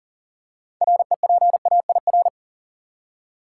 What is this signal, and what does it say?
Morse code, heard as a keyed tone, about 40 wpm, 700 Hz. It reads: REPAIR
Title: Six Letter Words - 40wpm